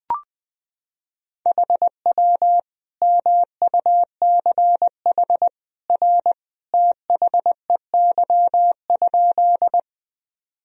Morse code, heard as a keyed tone, about 21 wpm, 700 Hz. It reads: HW MUCH R THEY?
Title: Sentences from Top 200 Words and Abbreviations 20wpm